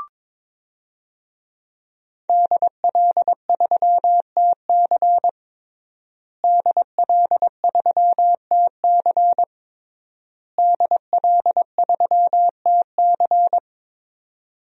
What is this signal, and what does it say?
Morse code, heard as a keyed tone, about 21 wpm, 700 Hz. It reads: DL3TC DL3TC DL3TC
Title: DXCC-100 Call Signs - 5 Characters - Repeated 3x - 22wpm